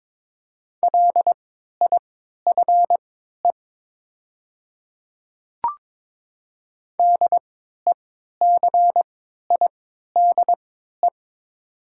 Morse code, heard as a keyed tone, about 21 wpm, spaced as for 12 wpm, 700 Hz. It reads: LIFE DECIDE
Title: 3x Character Spacing - Top 500 Words - 22wpm